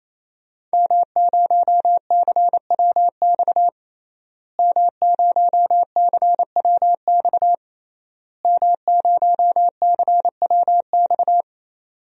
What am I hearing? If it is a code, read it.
Morse code, heard as a keyed tone, about 27 wpm, 700 Hz. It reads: M0CWX M0CWX M0CWX
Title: DXCC-100 Call Signs - 5 Characters - Repeated 3x - 28wpm